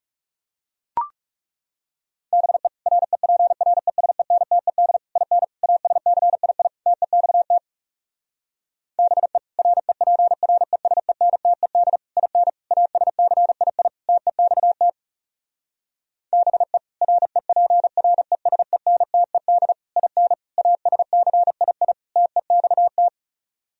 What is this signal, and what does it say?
Morse code, heard as a keyed tone, about 40 wpm, 700 Hz. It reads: BE REPRESENTED IN ASCII TEXT BE REPRESENTED IN ASCII TEXT BE REPRESENTED IN ASCII TEXT